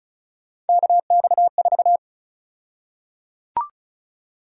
Morse code, heard as a keyed tone, about 34 wpm, 700 Hz. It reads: KX4
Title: USA Call Sign Prefixes - 35wpm